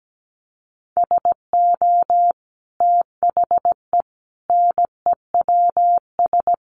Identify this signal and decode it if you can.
Morse code, heard as a keyed tone, about 17 wpm, 700 Hz. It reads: SO THE NEWS
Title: Sentences from Top 1200 Words 17wpm